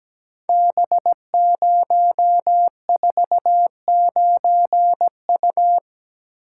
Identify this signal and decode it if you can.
Morse code, heard as a keyed tone, about 17 wpm, 700 Hz. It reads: B049U
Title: Code Groups - 5 Letter:Numbers - 17wpm